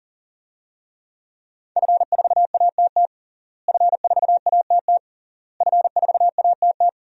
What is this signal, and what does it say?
Morse code, heard as a keyed tone, about 40 wpm, 700 Hz. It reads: F4ATT F4ATT F4ATT
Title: DXCC-100 Call Signs - 5 Characters - Repeated 3x - 40wpm